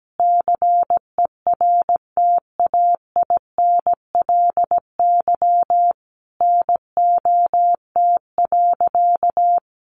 Morse code, heard as a keyed tone, about 17 wpm, 700 Hz. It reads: CERTAINLY NOT.
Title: Sentences from Top 900 Words 17wpm